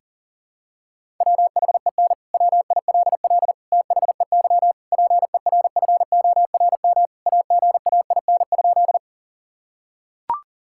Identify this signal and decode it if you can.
Morse code, heard as a keyed tone, about 40 wpm, 700 Hz. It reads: WHEN WILL THEY PERFORM AGAIN?
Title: Sentences from Top 1000 Words - Review Part 5 - 40wpm